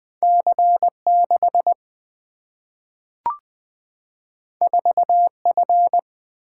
Morse code, heard as a keyed tone, about 20 wpm, 700 Hz. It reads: C 6 4 F
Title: Single-Letter Single-Number 20wpm